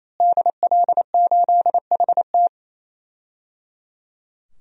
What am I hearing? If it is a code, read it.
Morse code, heard as a keyed tone, about 28 wpm, 700 Hz. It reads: DL8HT
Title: DXCC-100 Call Signs - 5 Characters - Repeated 3x - 28wpm